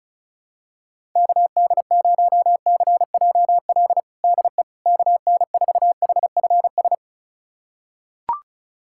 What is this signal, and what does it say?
Morse code, heard as a keyed tone, about 36 wpm, 700 Hz. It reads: KD0CJL DE KN4HFS
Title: USA Callsign DE Callsign 35wpm